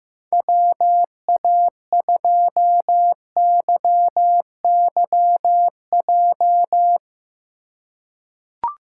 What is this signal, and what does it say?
Morse code, heard as a keyed tone, about 15 wpm, 700 Hz. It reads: WA2YYJ
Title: USA Call Signs - 6 Characters - 3x Character Spacing - 15wpm